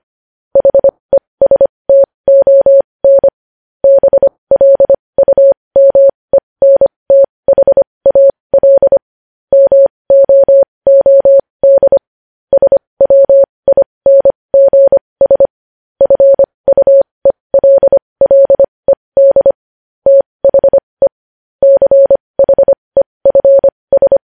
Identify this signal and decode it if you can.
Morse code, heard as a keyed tone, about 25 wpm, spaced as for 20 wpm, 550 Hz. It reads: HESTON BLUMENTHAL MOOD SWINGS FUELLED THE CHEFS
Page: News Headlines in Morse Code at 20 WPM